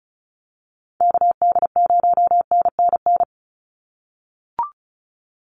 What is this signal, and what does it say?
Morse code, heard as a keyed tone, about 34 wpm, 700 Hz. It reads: KD0NNN
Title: USA Call Signs 35wpm